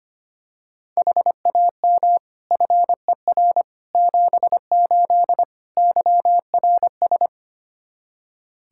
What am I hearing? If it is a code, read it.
Morse code, heard as a keyed tone, about 25 wpm, 700 Hz. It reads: HAM FER 78 YRS
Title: Practice QSO - Exchange 3 - More Info - 25wpm